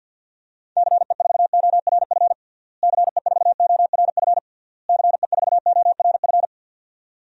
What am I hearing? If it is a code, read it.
Morse code, heard as a keyed tone, about 49 wpm, 700 Hz. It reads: CE4ORF CE4ORF CE4ORF
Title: DXCC-100 Call Signs - 6 Characters - Repeated 3x - 50wpm